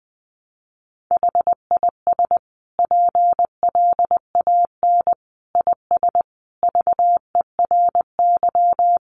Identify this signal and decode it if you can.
Morse code, heard as a keyed tone, about 19 wpm, 700 Hz. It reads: HIS PLAN IS VERY
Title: Sentences from Top 500 Words 20wpm